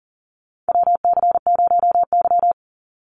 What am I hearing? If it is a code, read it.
Morse code, heard as a keyed tone, about 40 wpm, 700 Hz. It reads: WC0Y
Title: NA SOTA Chaser Callsigns 40wpm